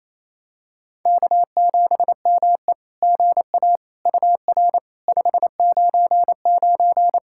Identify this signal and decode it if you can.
Morse code, heard as a keyed tone, about 29 wpm, 690 Hz. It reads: K7ME GA UR 599
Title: SOTA QSO Practice 28wpm